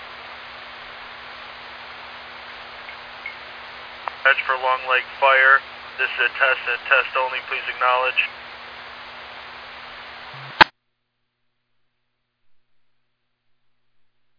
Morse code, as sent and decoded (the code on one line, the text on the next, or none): none